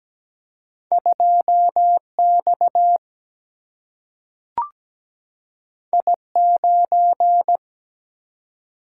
Morse code, heard as a keyed tone, about 17 wpm, 700 Hz. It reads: 2 X I 9
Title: Single-Letter Single-Number 17wpm